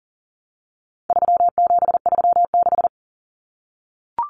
Morse code, heard as a keyed tone, about 40 wpm, 700 Hz. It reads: 3736
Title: Numbers - 4-Digit 40wpm